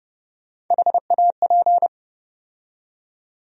Morse code, heard as a keyed tone, about 31 wpm, 700 Hz. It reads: HAP
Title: USA Call Sign Postfixes - 3 Letters 30wpm